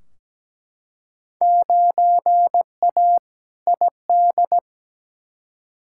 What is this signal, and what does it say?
Morse code, heard as a keyed tone, about 17 wpm, 700 Hz. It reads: 9A ID
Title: Field Day Exchanges Encore - 17wpm